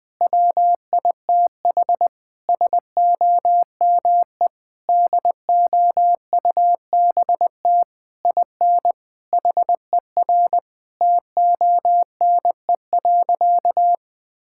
Morse code, heard as a keyed tone, about 21 wpm, 700 Hz. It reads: WITH SOME DOUBT IN HER TONE.
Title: Sentences from Top 1600 Words 20wpm